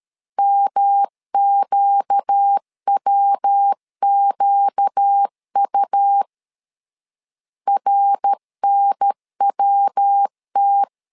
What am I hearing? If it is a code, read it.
Morse code, heard as a keyed tone, about 13 wpm, 800 Hz. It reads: MQWQU RNWT
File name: LEZIONI / 4 Q 12wpm 1833